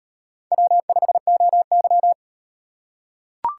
Morse code, heard as a keyed tone, about 37 wpm, 700 Hz. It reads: W5OY